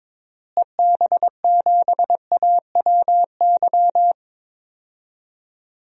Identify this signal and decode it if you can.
Morse code, heard as a keyed tone, about 21 wpm, 700 Hz. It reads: EB7AWY
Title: DXCC-100 Call Signs - 6 Characters - Repeated 2x - 22wpm